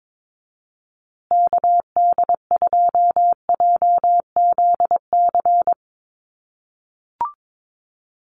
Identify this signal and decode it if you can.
Morse code, heard as a keyed tone, about 22 wpm, 700 Hz. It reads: KD2JZC
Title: USA Call Signs Repeated 2x 22wpm